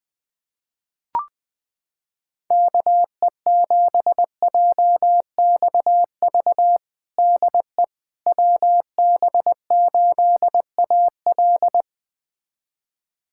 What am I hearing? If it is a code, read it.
Morse code, heard as a keyed tone, about 20 wpm, 700 Hz. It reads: KE7JXV DE WB8AL
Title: USA Callsign DE Callsign 20wpm